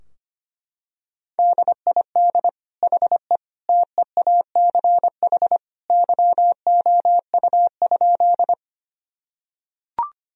Morse code, heard as a keyed tone, about 25 wpm, 700 Hz. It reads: DID HE TEACH YOU?
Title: Sentences from Top 1000 Words - Review Part 2 - 25wpm